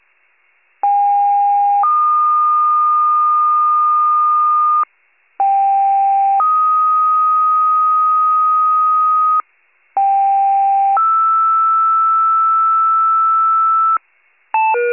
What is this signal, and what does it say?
Two-tone page, heard as a one-second tone, then a three-second tone, about 810 then 1220 Hz, about 790 then 1250 Hz, about 790 then 1320 Hz. It is heard from 0.8 to 14.0 s.